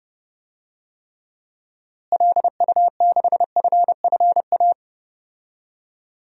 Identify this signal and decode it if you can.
Morse code, heard as a keyed tone, about 30 wpm, 700 Hz. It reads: LU6FFA
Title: DXCC-100 Call Signs - 6 Characters - Repeated 3x - 30wpm